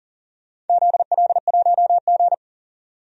Morse code, heard as a keyed tone, about 40 wpm, 700 Hz. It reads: ZL1G
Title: DXCC-100 Call Signs - 4 Characters - 40wpm